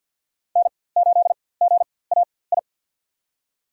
Morse code, heard as a keyed tone, about 53 wpm, spaced as for 24 wpm, 690 Hz. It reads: N8GAI
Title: USA Call Signs - 5 Characters - 4x Character Spacing - 50wpm